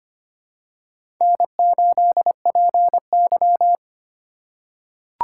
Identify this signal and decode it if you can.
Morse code, heard as a keyed tone, about 25 wpm, 700 Hz. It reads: N8PY
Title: USA Call Signs - 4 Characters - 3x Character Spacing - 25wpm